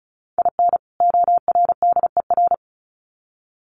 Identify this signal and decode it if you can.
Morse code, heard as a keyed tone, about 34 wpm, 700 Hz. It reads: IN ORDER
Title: Sets of 2 Words 35wpm